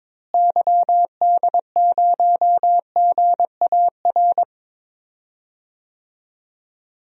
Morse code, heard as a keyed tone, about 22 wpm, 700 Hz. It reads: YD0GAR
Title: DXCC-100 Call Signs - 6 Characters - Repeated 2x - 22wpm